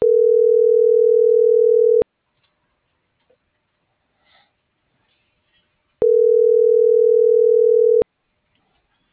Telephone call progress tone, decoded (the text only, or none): ringback tone